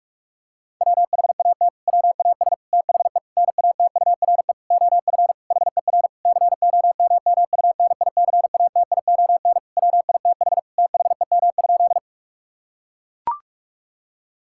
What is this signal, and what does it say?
Morse code, heard as a keyed tone, about 41 wpm, 700 Hz. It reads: WHAT WAS THE NATURE OF HER COMMUNICATION WITH THEM?
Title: Sentences from Top 1000 Words - Review Part 6 - 45wpm